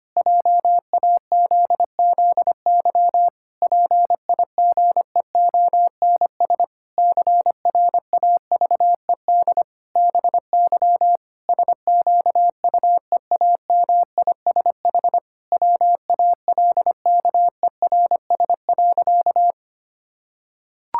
Morse code, heard as a keyed tone, about 25 wpm, 700 Hz. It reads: JAZZY PIGEONS CRAVED BY SQUEAMISH WALKERS.
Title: Pangram Challenge v2 - 25wpm